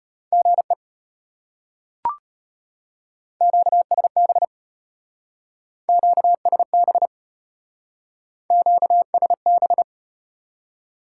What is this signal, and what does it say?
Morse code, heard as a keyed tone, about 33 wpm, 700 Hz. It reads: GE QSB QSB QSB